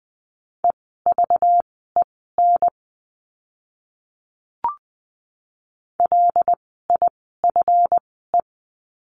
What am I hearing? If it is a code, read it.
Morse code, heard as a keyed tone, about 20 wpm, spaced as for 15 wpm, 700 Hz. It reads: EVEN LIFE
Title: 230 Lesson 37f - Letter x - Words 20wpm